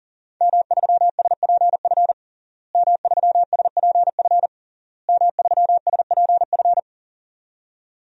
Morse code, heard as a keyed tone, about 40 wpm, 700 Hz. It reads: M3SPF M3SPF M3SPF
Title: DXCC-100 Call Signs - 5 Characters - Repeated 3x - 40wpm